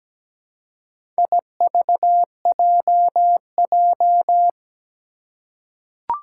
Morse code, heard as a keyed tone, about 17 wpm, 700 Hz. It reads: IVJJ
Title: Four Challenging Letter-Number Combinations 17wpm